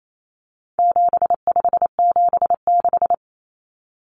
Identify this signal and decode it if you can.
Morse code, heard as a keyed tone, about 28 wpm, 700 Hz. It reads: 7576
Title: Common Cut Numbers - 4-Digit 28wpm